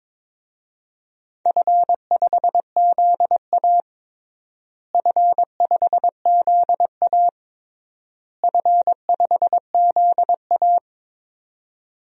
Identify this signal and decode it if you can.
Morse code, heard as a keyed tone, about 22 wpm, 700 Hz. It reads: F5ZA F5ZA F5ZA
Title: DXCC-100 Call Signs - 4 Characters - Repeated 3x - 22wpm